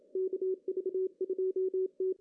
Morse code, heard as a keyed tone, about 30 wpm, 380 Hz. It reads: KV2T
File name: CW / cqww-cw-k5tr